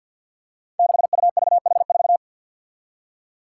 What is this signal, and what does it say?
Morse code, heard as a keyed tone, about 51 wpm, 700 Hz. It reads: 6UVH4
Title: Five Challenging Letter-Number Combinations 50wpm